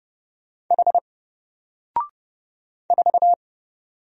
Morse code, heard as a keyed tone, about 30 wpm, 700 Hz. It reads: H 4
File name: Single-Letter-Number-30wpm